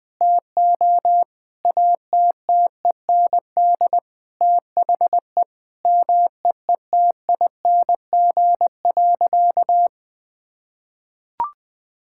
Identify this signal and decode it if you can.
Morse code, heard as a keyed tone, about 21 wpm, 700 Hz. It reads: TO ATTEND THE MEETING.
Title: Sentences with 6-Word Phrases - 20wpm